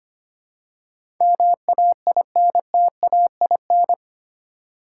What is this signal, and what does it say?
Morse code, heard as a keyed tone, about 25 wpm, 700 Hz. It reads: MAINTAIN
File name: Eight Letter Words - 25wpm